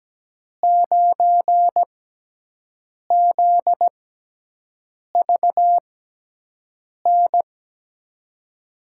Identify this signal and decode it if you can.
Morse code, heard as a keyed tone, about 17 wpm, 700 Hz. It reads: 9 Z V N
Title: USA Call Signs - 6 Characters - 6x Character Spacing - 17wpm